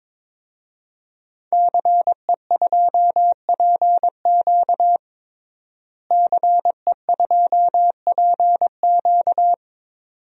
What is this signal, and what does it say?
Morse code, heard as a keyed tone, about 22 wpm, 700 Hz. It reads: CE2PQ CE2PQ
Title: DXCC-100 Call Signs - 5 Characters - Repeated 2x - 22wpm